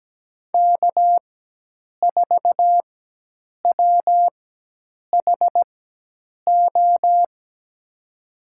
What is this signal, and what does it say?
Morse code, heard as a keyed tone, about 17 wpm, spaced as for 8 wpm, 700 Hz. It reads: K4WHO
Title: USA Call Signs - 4x Character Spacing - 17wpm